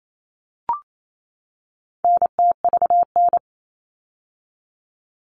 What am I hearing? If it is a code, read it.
Morse code, heard as a keyed tone, about 28 wpm, 700 Hz. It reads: NTVN
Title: All Cut Numbers - 4-Digit 28wpm